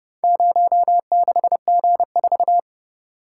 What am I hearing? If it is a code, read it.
Morse code, heard as a keyed tone, about 31 wpm, 700 Hz. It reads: 06G4